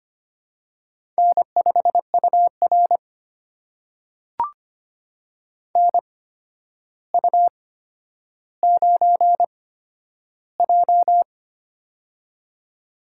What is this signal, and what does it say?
Morse code, heard as a keyed tone, about 25 wpm, spaced as for 11 wpm, 700 Hz. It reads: N5UR NU9J
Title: USA Call Signs - 4 Characters - 8x Character Spacing - 25wpm